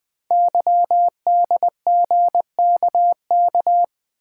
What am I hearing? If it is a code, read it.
Morse code, heard as a keyed tone, about 20 wpm, 700 Hz. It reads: YDGKK
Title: Code Groups - 5 Letters - 20wpm